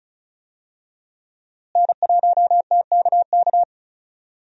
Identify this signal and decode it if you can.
Morse code, heard as a keyed tone, about 36 wpm, 700 Hz. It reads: N1TKK
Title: USA Call Signs - 5 Characters - Repeated 3x - 35wpm